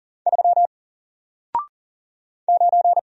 Morse code, heard as a keyed tone, about 40 wpm, 700 Hz. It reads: 3 9
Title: Single Character (Letter:Number:Punctuation:Prosign) Alt 40wpm